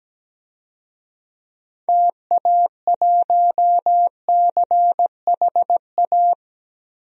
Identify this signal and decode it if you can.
Morse code, heard as a keyed tone, about 17 wpm, 700 Hz. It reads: TA1CHA
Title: DXCC-100 Call Signs - 6 Characters - 17wpm